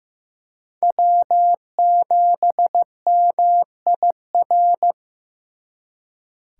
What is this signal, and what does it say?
Morse code, heard as a keyed tone, about 18 wpm, 700 Hz. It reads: W7MIR
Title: Speed Racing - USA 1x3 Call Signs 15wpm